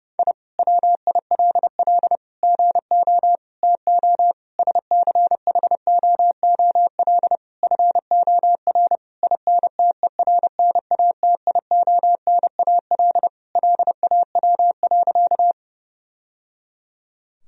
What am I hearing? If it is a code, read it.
Morse code, heard as a keyed tone, about 32 wpm, 700 Hz. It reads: I WILL GO TO SCHOOL FOR INTERNATIONAL LAW.
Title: Sentences from Top 1000 Words - Review Part 4 - 30wpm